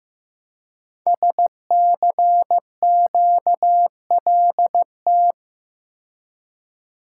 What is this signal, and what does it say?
Morse code, heard as a keyed tone, about 15 wpm, 700 Hz. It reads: SCQLT
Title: Code Groups - 5 Letters - 15wpm